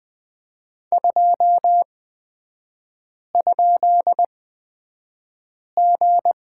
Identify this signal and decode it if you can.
Morse code, heard as a keyed tone, about 20 wpm, spaced as for 5 wpm, 700 Hz. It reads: 2?G
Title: 229 Lesson 37e - Letter x - Three Characters 20wpm